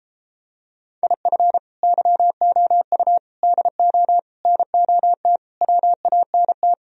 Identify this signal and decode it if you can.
Morse code, heard as a keyed tone, about 33 wpm, 700 Hz. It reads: IF YOU DO NOT WANT